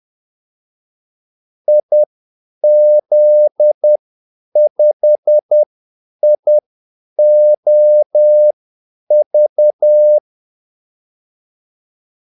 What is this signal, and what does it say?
Morse code, heard as a keyed tone, about 10 wpm, spaced as for 8 wpm, 600 Hz. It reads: IZ5IOV
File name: Callsigns-10-8wpm